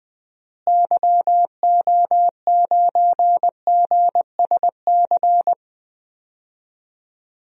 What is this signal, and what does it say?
Morse code, heard as a keyed tone, about 21 wpm, 700 Hz. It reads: YO9GSC
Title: DXCC-100 Call Signs - 6 Characters - Repeated 3x - 20wpm